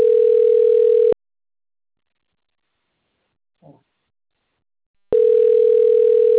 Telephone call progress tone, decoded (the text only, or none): ringback tone